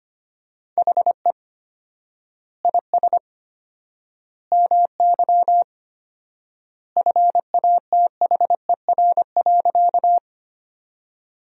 Morse code, heard as a keyed tone, about 25 wpm, 700 Hz. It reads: HE IS MY FATHER.
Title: 4x Word-Spacing - Sentences from Top 300 Words - 25wpm